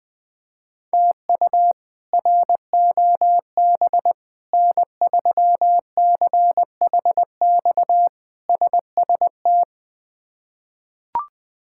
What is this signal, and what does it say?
Morse code, heard as a keyed tone, about 21 wpm, 700 Hz. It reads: TU ROB N3CHX SST
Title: K1USN SST QSO Practice - 20wpm